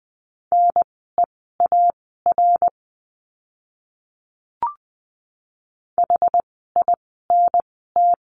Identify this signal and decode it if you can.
Morse code, heard as a keyed tone, about 20 wpm, spaced as for 14 wpm, 700 Hz. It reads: NEAR HINT
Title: 061 Lesson 11f - Letter h - Words 20wpm